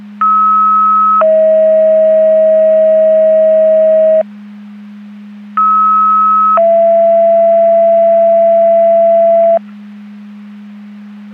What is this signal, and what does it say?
Two-tone page, heard as a one-second tone, then a three-second tone, about 1290 then 640 Hz, about 1290 then 670 Hz. It is heard from 0.2 to 9.6 s.
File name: Portage County 154.3850 Fire EMS Paging